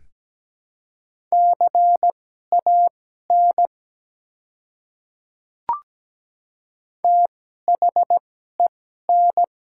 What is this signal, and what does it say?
Morse code, heard as a keyed tone, about 17 wpm, spaced as for 12 wpm, 700 Hz. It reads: CAN THEN
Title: 2x Character Spacing - Top 100 Words - 17wpm